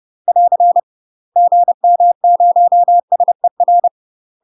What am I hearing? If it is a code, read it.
Morse code, heard as a keyed tone, about 30 wpm, 700 Hz. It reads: <AR> GM0SER
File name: UK MASTER QSOs TRAINING / 30WPM / UK MASTER QSOs 0004